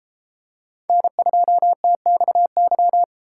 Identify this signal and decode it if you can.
Morse code, heard as a keyed tone, about 32 wpm, 700 Hz. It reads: N2TXY